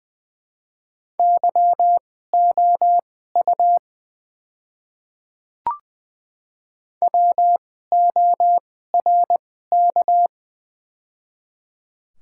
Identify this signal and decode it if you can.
Morse code, heard as a keyed tone, about 21 wpm, spaced as for 15 wpm, 700 Hz. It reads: YOU WORK
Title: 2x Character Spacing - Top 200 Words - 20wpm